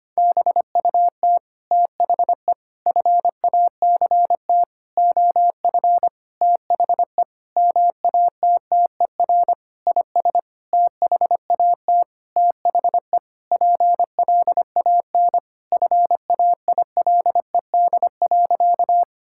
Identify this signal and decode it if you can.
Morse code, heard as a keyed tone, about 25 wpm, 700 Hz. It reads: BUT THE FACT OF THE MATTER IS THAT THE PLAN FAILED.
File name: practice / 25wpm / Sentences with 9-Word Phrases - 25wpm